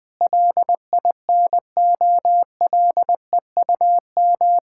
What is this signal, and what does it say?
Morse code, heard as a keyed tone, about 20 wpm, 700 Hz. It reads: LINOLEUM
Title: Pangram Challenge v2 - 20wpm